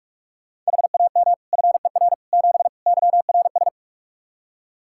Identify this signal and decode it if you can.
Morse code, heard as a keyed tone, about 46 wpm, 700 Hz. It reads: HAM FER 7 YRS
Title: Practice QSO - Exchange 3 - More Info - 45wpm